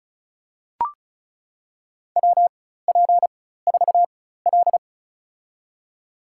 Morse code, heard as a keyed tone, about 36 wpm, spaced as for 16 wpm, 700 Hz. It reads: WP4L
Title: USA Call Signs - 4 Characters - 4x Character Spacing - 35wpm